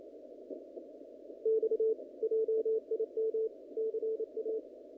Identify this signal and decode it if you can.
Morse code, heard as a keyed tone, about 29 wpm, 440 Hz. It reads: X JIM CA
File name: contests / naqp-cw-k5tr-op-k5pi- -1900